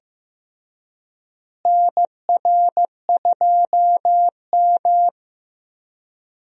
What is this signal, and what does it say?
Morse code, heard as a keyed tone, about 15 wpm, 700 Hz. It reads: NR2M
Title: USA Call Signs - 4 Characters - Repeated 3x - 15wpm